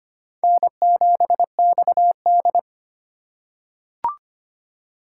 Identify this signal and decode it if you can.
Morse code, heard as a keyed tone, about 25 wpm, 700 Hz. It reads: N7XD
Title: USA Call Signs - 4 Characters - 3x Character Spacing - 25wpm